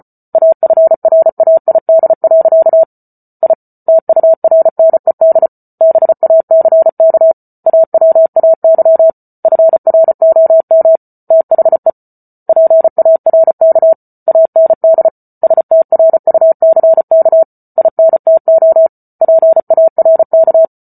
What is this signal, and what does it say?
Morse code, heard as a keyed tone, about 34 wpm, 650 Hz. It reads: AFRAID. I TURNED BACK AWAY FROM THE PARK AND STRUCK INTO PARK